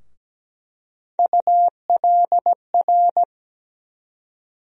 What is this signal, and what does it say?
Morse code, heard as a keyed tone, about 17 wpm, 700 Hz. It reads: ULR
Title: USA Call Sign Postfixes - 3 Letters 17wpm